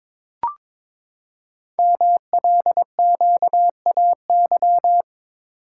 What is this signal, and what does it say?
Morse code, heard as a keyed tone, about 22 wpm, 700 Hz. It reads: MLQAY
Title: Code Groups - 5 Letters - 22wpm